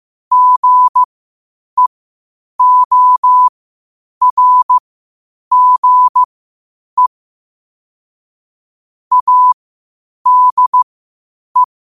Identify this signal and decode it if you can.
Morse code, heard as a keyed tone, about 15 wpm, spaced as for 8 wpm, 1000 Hz. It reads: GEORGE ADE
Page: Quotes for Sat, 16 Aug 2025 in Morse Code at 8 words per minute.